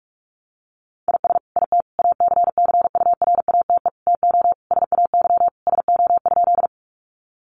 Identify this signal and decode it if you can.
Morse code, heard as a keyed tone, about 46 wpm, 700 Hz. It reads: IS IT ACCURATE TO SAY SO?
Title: Sentences from Top 1200 Words 45wpm